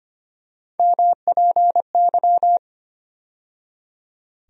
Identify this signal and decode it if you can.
Morse code, heard as a keyed tone, about 25 wpm, 700 Hz. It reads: MPY
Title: USA Call Sign Postfixes - 3 Letters 25wpm